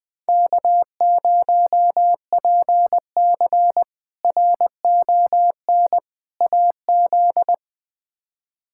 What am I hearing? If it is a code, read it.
Morse code, heard as a keyed tone, about 20 wpm, 700 Hz. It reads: K0PC RON AZ
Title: K1USN SST QSO Practice - 20wpm